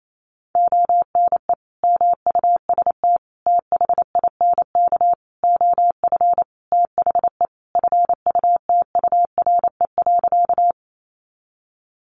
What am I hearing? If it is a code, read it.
Morse code, heard as a keyed tone, about 28 wpm, 700 Hz. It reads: ONE MUST THINK OF THE FUTURE.
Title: Sentences from Top 700 Words 28wpm